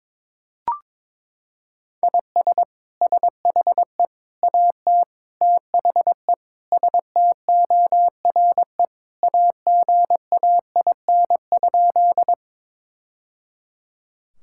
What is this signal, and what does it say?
Morse code, heard as a keyed tone, about 22 wpm, 700 Hz. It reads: IS SHE AT THE STORE AGAIN?
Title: Sentences from Top 1000 Words - Review Part 5 - 22wpm